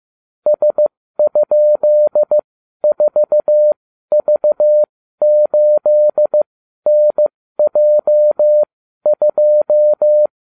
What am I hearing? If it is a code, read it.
Morse code, heard as a keyed tone, about 15 wpm, spaced as for 12 wpm, 600 Hz. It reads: S?4V8NJ2